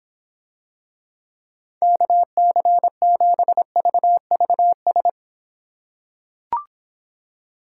Morse code, heard as a keyed tone, about 25 wpm, 700 Hz. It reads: KC7VVS